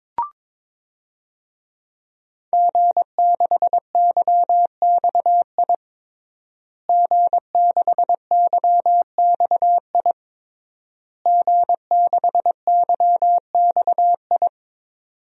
Morse code, heard as a keyed tone, about 22 wpm, 700 Hz. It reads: G6YXI G6YXI G6YXI
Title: DXCC-100 Call Signs - 5 Characters - Repeated 3x - 22wpm